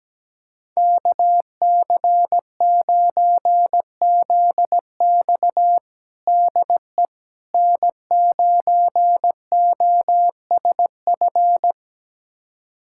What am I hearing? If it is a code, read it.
Morse code, heard as a keyed tone, about 17 wpm, 700 Hz. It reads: KC9ZX DE N9OSF
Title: USA Callsign DE Callsign - 5 Characters - 17wpm